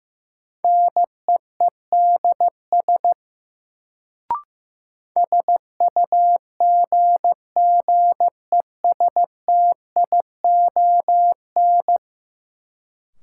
Morse code, heard as a keyed tone, about 15 wpm, 700 Hz. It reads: NEEDS SUGGESTION
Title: Sentences from Top 1600 Words - New Word Drill Alt - 15wpm